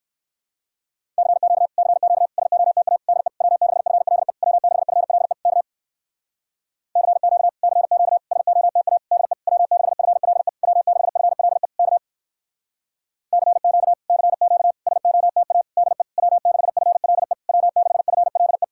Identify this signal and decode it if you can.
Morse code, heard as a keyed tone, about 64 wpm, 700 Hz. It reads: CQ CQ SOTA DE W6FLE W6FLE K CQ CQ SOTA DE W6FLE W6FLE K CQ CQ SOTA DE W6FLE W6FLE